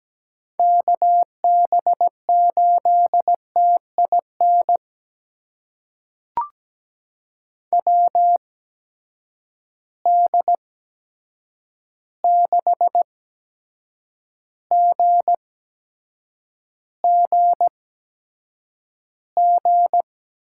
Morse code, heard as a keyed tone, about 17 wpm, spaced as for 7 wpm, 700 Hz. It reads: KB8TIN WD6GGG
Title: USA Call Signs - 6 Characters - 8x Character Spacing - 17wpm